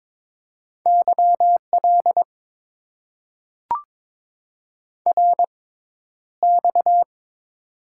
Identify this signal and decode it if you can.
Morse code, heard as a keyed tone, about 22 wpm, spaced as for 11 wpm, 700 Hz. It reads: YL RX
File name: Top 90 QSO Elements - 6x Character Spacing - 22wpm